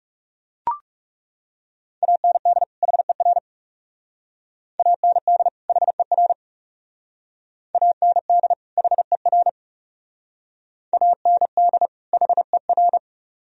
Morse code, heard as a keyed tone, about 38 wpm, 700 Hz. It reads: AND HER AND HER AND HER AND HER
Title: Speed-Racing - Sets of 2 Words 30wpm